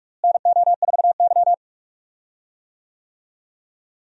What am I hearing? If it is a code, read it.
Morse code, heard as a keyed tone, about 41 wpm, 700 Hz. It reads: NO4Y
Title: USA Call Signs - 4 Characters - Repeated 2x - 45wpm